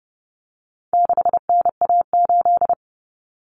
Morse code, heard as a keyed tone, about 30 wpm, 700 Hz. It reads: 6NA8
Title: Common Cut Numbers - 4-Digit 30wpm